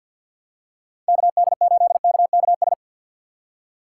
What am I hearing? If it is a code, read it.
Morse code, heard as a keyed tone, about 53 wpm, 700 Hz. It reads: KD8KKS
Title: USA Call Signs - 6x Character Spacing - 50wpm